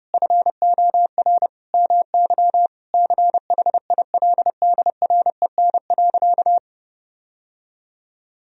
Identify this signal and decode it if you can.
Morse code, heard as a keyed tone, about 30 wpm, 700 Hz. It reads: FOR MY CHILDREN.
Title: Sentences from Top 1000 Words - Review Part 3 - 30wpm